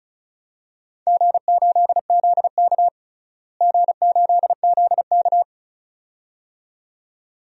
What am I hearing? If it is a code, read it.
Morse code, heard as a keyed tone, about 34 wpm, 700 Hz. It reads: G8ZK G8ZK
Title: DXCC-100 Call Signs - 4 Characters - Repeated 2x - 35wpm